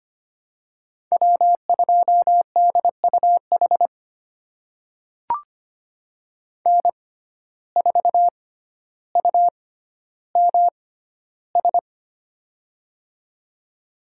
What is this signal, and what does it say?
Morse code, heard as a keyed tone, about 25 wpm, spaced as for 13 wpm, 700 Hz. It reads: W2DUH N4UMS
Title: USA Call Signs - 5 Characters - 6x Character Spacing - 25wpm